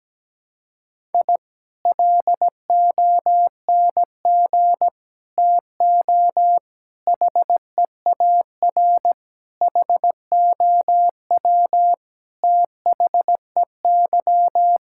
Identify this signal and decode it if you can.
Morse code, heard as a keyed tone, about 17 wpm, 700 Hz. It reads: I LONG TO HEAR HOW THEY
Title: Sentences from Top 1000 Words - Review Part 5 - 17wpm